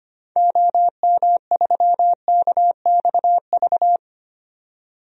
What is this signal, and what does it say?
Morse code, heard as a keyed tone, about 25 wpm, 700 Hz. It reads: OM3KXV
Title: DXCC-100 Call Signs - 6 Characters - 25wpm